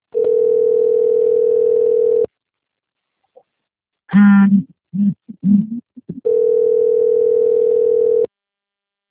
Ringback tone (North America), 440 + 480 Hz.
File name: rg-1600-unknown- - - .58786